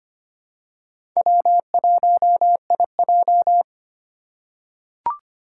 Morse code, heard as a keyed tone, about 25 wpm, 700 Hz. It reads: W1IJ
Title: USA Call Signs - 4 Characters - 4x Character Spacing - 25wpm